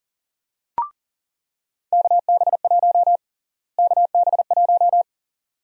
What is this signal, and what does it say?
Morse code, heard as a keyed tone, about 40 wpm, 700 Hz. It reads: KB1 KB1
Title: USA Call Sign Prefixes - Repeated 2x - 40wpm